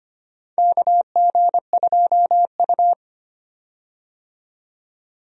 Morse code, heard as a keyed tone, about 25 wpm, 700 Hz. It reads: KG2U
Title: USA Call Signs - 4 Characters - 25wpm